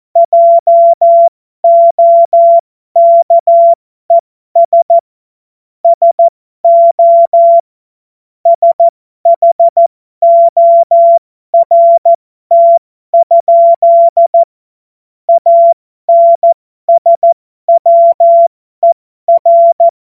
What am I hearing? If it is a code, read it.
Morse code, heard as a keyed tone, about 14 wpm, spaced as for 12 wpm, 680 Hz. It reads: JOKES SO SHORT? ANSWER
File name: Morsekurs / Witze